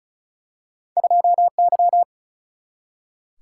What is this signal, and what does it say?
Morse code, heard as a keyed tone, about 36 wpm, 700 Hz. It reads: 2 Y
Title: Single-Letter Single-Number 35wpm